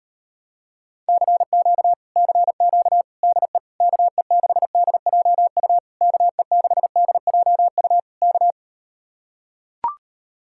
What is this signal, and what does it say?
Morse code, heard as a keyed tone, about 39 wpm, 700 Hz. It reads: CQ CQ DE KE6DJU KE6DJU K